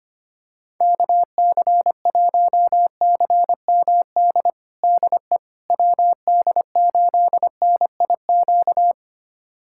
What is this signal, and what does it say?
Morse code, heard as a keyed tone, about 25 wpm, 700 Hz. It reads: KC1CMD DE WD8NIQ
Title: USA Callsign DE Callsign 25wpm